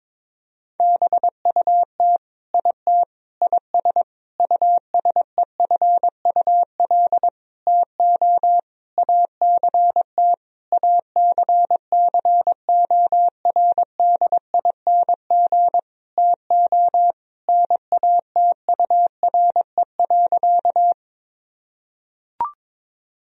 Morse code, heard as a keyed tone, about 22 wpm, 700 Hz. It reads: BUT IT IS USEFUL TO ACT ACCORDING TO NATURE.
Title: Sentences from Top 1000 Words - Review Part 4 - 22wpm